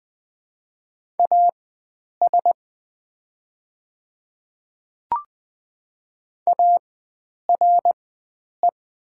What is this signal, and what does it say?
Morse code, heard as a keyed tone, about 20 wpm, spaced as for 9 wpm, 700 Hz. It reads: AS ARE
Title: 4x Character Spacing - Top 100 Words - 20wpm